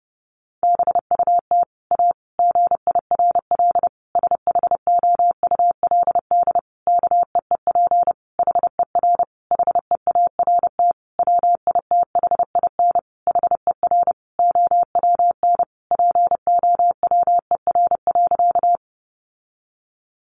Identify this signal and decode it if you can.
Morse code, heard as a keyed tone, about 30 wpm, 700 Hz. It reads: BUT A GIRL SHOULD KEEP HER HEART WITHIN HER OWN POWER.
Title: Sentences from Top 1200 Words 30wpm